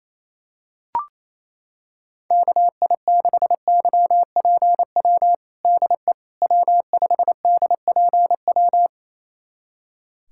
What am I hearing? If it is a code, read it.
Morse code, heard as a keyed tone, about 27 wpm, 700 Hz. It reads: KI6YPW DE W5DPW
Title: USA Callsign DE Callsign 28wpm